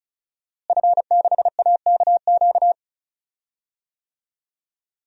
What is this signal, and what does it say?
Morse code, heard as a keyed tone, about 36 wpm, 700 Hz. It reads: F6AKQ
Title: DXCC-100 Call Signs - 5 Characters - Repeated 3x - 35wpm